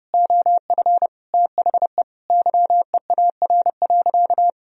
Morse code, heard as a keyed tone, about 32 wpm, 700 Hz. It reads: OF THE YEAR.
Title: Sentences from Top 1000 Words - Review Part 1 - 30wpm